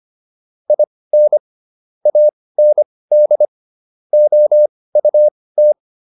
Morse code, heard as a keyed tone, about 25 wpm, spaced as for 18 wpm, 600 Hz. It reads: IN AND OUT